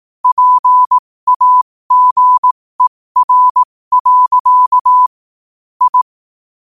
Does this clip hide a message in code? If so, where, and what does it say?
Yes: Morse code from 0.2 to 6.0 s, heard as a keyed tone, about 18 wpm, spaced as for 15 wpm, 1000 Hz. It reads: PAGER. I